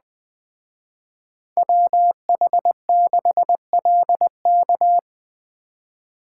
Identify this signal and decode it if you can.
Morse code, heard as a keyed tone, about 20 wpm, 700 Hz. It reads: WH6LK
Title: Speed Racing - USA 2x2 Call Signs 17wpm